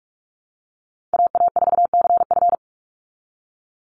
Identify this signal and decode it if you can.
Morse code, heard as a keyed tone, about 44 wpm, 700 Hz. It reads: AA4CF
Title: Call Signs Repeated 2x - 45wpm